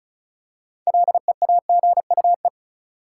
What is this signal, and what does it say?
Morse code, heard as a keyed tone, about 36 wpm, 700 Hz. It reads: LEAGUE
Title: Six Letter Words - 35wpm